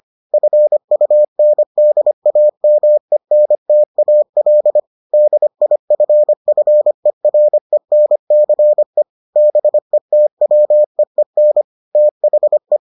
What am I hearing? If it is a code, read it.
Morse code, heard as a keyed tone, about 25 wpm, 600 Hz. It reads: FUNDAMENTAL DIFFERENCE BETWEEN THE